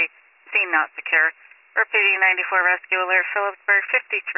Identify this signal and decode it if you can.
Emergency Alert System (SAME) tones, heard as harsh buzzing bursts, with speech. It is heard from 1.0 to 4.1 s.